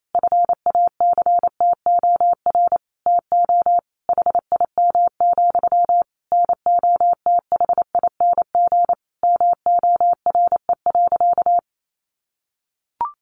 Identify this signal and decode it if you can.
Morse code, heard as a keyed tone, about 28 wpm, 700 Hz. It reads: FACTOR TO HIM, NOTHING MORE.
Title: Sentences from Top 1300 Words 28wpm